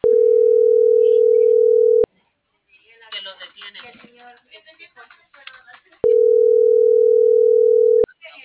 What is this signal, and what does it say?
Ringback tone (North America), 440 + 480 Hz.